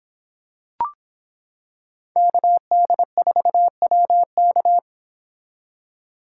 Morse code, heard as a keyed tone, about 26 wpm, 700 Hz. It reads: KD4WK